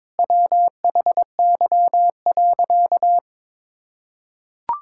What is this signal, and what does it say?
Morse code, heard as a keyed tone, about 22 wpm, 700 Hz. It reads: WHY.
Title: Sentences from Top 1600 Words 22wpm